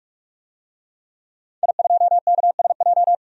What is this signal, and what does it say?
Morse code, heard as a keyed tone, about 44 wpm, 700 Hz. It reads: I2KSJ
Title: DXCC-100 Call Signs - 5 Characters - 45wpm